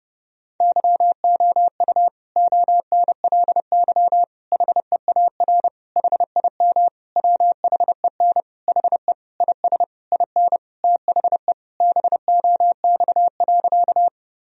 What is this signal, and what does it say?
Morse code, heard as a keyed tone, about 32 wpm, 700 Hz. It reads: YOU ONLY HEAR HIM WHEN HE IS IN THE BOX.
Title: Sentences from Top 1000 Words - Review Part 2 - 30wpm